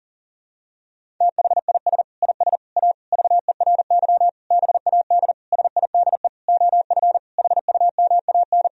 Morse code, heard as a keyed tone, about 40 wpm, 700 Hz. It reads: THIS IS A VERY BAD SIDE OF HUMAN
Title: Sentences from Top 1000 Words - Review Part 3 - 40wpm